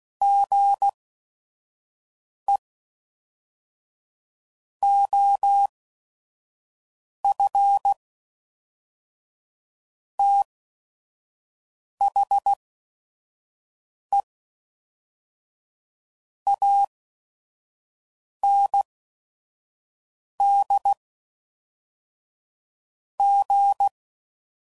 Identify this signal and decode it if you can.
Morse code, heard as a keyed tone, about 16 wpm, spaced as for 5 wpm, 800 Hz. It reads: GE OF THE AND G
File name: CW Code Groups @5WPM